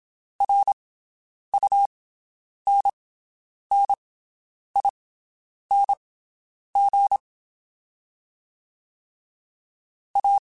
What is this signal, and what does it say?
Morse code, heard as a keyed tone, about 27 wpm, spaced as for 9 wpm, 800 Hz. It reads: RUNNING A
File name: QSOs Spaced